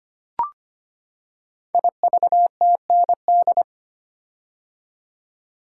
Morse code, heard as a keyed tone, about 25 wpm, 700 Hz. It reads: IVTND
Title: Code Groups - 5 Letters - 25wpm